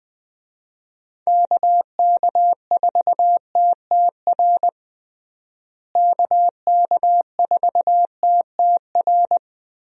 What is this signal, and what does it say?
Morse code, heard as a keyed tone, about 20 wpm, 700 Hz. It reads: KK4TTR KK4TTR
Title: USA Call Signs - 6 Characters - Repeated 2x - 20wpm